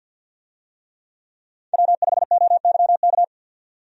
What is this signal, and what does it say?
Morse code, heard as a keyed tone, about 52 wpm, 700 Hz. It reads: W5OYK
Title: USA Call Signs - 5 Characters - 50wpm